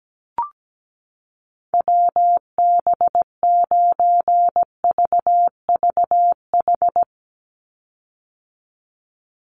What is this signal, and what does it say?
Morse code, heard as a keyed tone, about 17 wpm, 700 Hz. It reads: WB9VVH
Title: USA Call Signs 17wpm